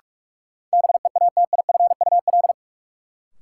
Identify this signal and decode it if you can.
Morse code, heard as a keyed tone, about 46 wpm, 700 Hz. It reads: BEATIFUL
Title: Sentences from Top 1300 Words - New Word Drill - 45wpm